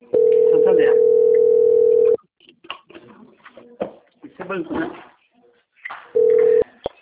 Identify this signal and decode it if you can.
Ringback tone (North America), 440 + 480 Hz, with speech.
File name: out- - - - - .11041